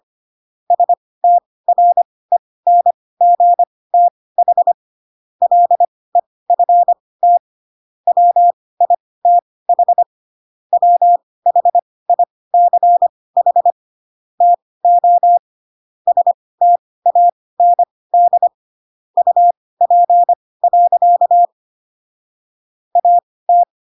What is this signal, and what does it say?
Morse code, heard as a keyed tone, about 25 wpm, spaced as for 18 wpm, 700 Hz. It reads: STRENGTH LEFT WITH WHICH TO STAND UP. AT
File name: Pinocchio / Pinocchio ch2 12 25 ik1hge